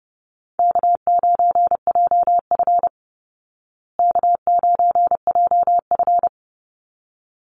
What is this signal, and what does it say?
Morse code, heard as a keyed tone, about 30 wpm, 700 Hz. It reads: K9JF K9JF
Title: Easy Call Signs Repeated 2x - 30wpm